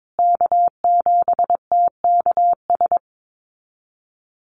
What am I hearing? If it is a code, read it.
Morse code, heard as a keyed tone, about 22 wpm, 700 Hz. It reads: K7TKS
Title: USA Call Signs Repeated 3x 22wpm